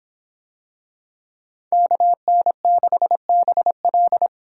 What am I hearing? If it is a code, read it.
Morse code, heard as a keyed tone, about 26 wpm, 700 Hz. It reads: KN6BL